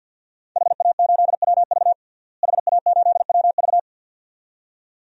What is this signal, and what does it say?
Morse code, heard as a keyed tone, about 52 wpm, 700 Hz. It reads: HA8WV HA8WV
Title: DXCC-100 Call Signs - 5 Characters - Repeated 2x - 50wpm